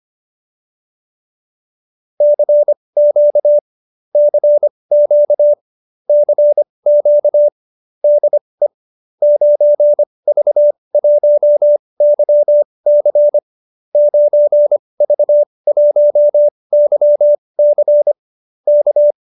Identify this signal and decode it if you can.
Morse code, heard as a keyed tone, about 25 wpm, spaced as for 20 wpm, 590 Hz. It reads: CQ CQ CQ DE 9V1YC 9V1YC K
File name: qso207b 20